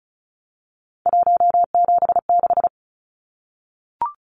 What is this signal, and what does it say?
Morse code, heard as a keyed tone, about 34 wpm, 700 Hz. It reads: 176
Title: Numbers - 3-Digit 35wpm